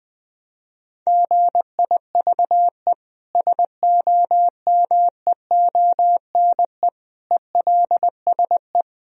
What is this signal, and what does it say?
Morse code, heard as a keyed tone, about 20 wpm, 700 Hz. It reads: GIVE SOMEONE ELSE
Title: Sentences from Top 1000 Words - Review Part 4 - 20wpm